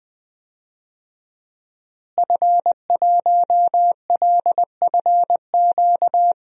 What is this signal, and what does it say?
Morse code, heard as a keyed tone, about 20 wpm, 700 Hz. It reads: F1LFQ
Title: DXCC-100 Call Signs - 5 Characters - Repeated 3x - 20wpm